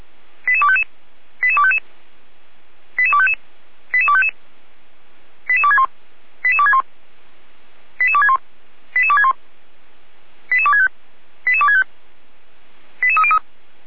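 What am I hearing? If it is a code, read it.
Five-tone selective call (ZVEI), heard as quick runs of five tones, tones of about 70 ms. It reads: 88255 88255 88255 88255 88261 88261 88261 88261 88265 88265 88303
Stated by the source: ZVEI five-tone selective call received by SDR, logged by feuerwehr-orlamuende (fire brigade Orlamünde).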